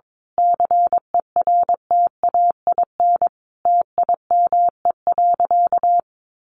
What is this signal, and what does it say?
Morse code, heard as a keyed tone, about 22 wpm, 700 Hz. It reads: CERTAIN TIME.
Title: Sentences from Top 1000 Words 22wpm